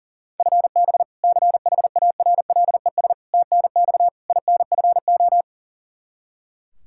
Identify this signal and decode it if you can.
Morse code, heard as a keyed tone, about 40 wpm, 700 Hz. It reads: FB CHARLES TNX INFO
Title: Practice QSO - Exchange 3 - More Info - 40wpm